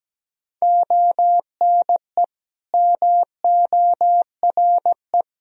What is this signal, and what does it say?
Morse code, heard as a keyed tone, about 17 wpm, 700 Hz. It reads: ONE MORE
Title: Sentences from Top 1000 Words - Review Part 2 - 17wpm